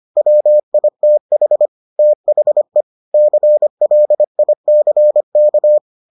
Morse code, heard as a keyed tone, about 25 wpm, 600 Hz. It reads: WITH THE CLICK